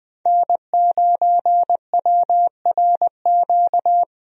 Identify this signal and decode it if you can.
Morse code, heard as a keyed tone, about 20 wpm, 700 Hz. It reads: N9WRQ
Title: USA Call Signs - 5 Characters - 20wpm